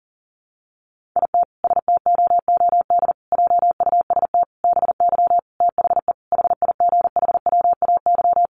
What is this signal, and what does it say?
Morse code, heard as a keyed tone, about 37 wpm, 700 Hz. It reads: IT STOOD JUST BY THE HIGHWAY
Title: Sentences from Top 1400 Words 40wpm